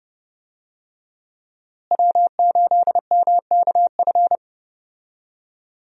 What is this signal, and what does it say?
Morse code, heard as a keyed tone, about 30 wpm, 700 Hz. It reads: W8MKF
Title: USA Call Signs - 5 Characters - Repeated 2x - 30wpm